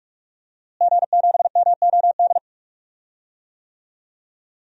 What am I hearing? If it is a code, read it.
Morse code, heard as a keyed tone, about 45 wpm, 700 Hz. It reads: G7MOD
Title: DXCC-100 Call Signs - 5 Characters - Repeated 3x - 45wpm